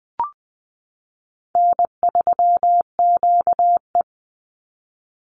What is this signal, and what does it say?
Morse code, heard as a keyed tone, about 20 wpm, 700 Hz. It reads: N3QE
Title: Easy Call Signs Repeated 3x - 20wpm